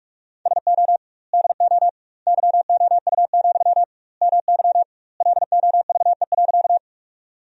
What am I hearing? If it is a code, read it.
Morse code, heard as a keyed tone, about 44 wpm, 700 Hz. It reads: SO DO YOU, MY LOVE.
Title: 2x Word-Spacing - Sentences from Top 400 Words - 45wpm